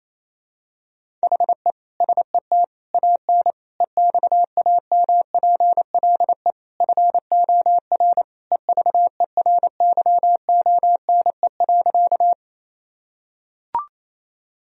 Morse code, heard as a keyed tone, about 28 wpm, 700 Hz. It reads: HE SET AN EXAMPLE FOR EVERYONE.
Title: Sentences from Top 1000 Words - Review Part 1 - 28wpm